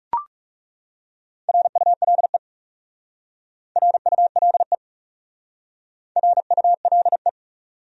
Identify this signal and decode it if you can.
Morse code, heard as a keyed tone, about 39 wpm, 700 Hz. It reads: RULE RULE RULE